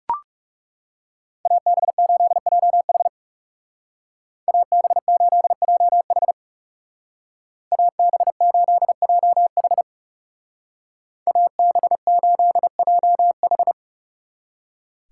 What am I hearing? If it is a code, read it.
Morse code, heard as a keyed tone, about 39 wpm, 700 Hz. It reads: AB8JH AB8JH AB8JH AB8JH